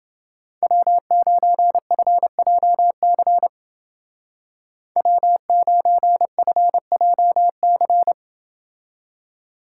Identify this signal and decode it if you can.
Morse code, heard as a keyed tone, about 29 wpm, 700 Hz. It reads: W9FJC W9FJC